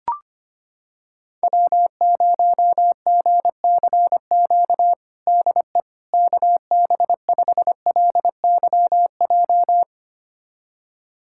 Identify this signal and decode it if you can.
Morse code, heard as a keyed tone, about 25 wpm, 700 Hz. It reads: W0GCQ DE KB5LYJ
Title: USA Callsign DE Callsign 25wpm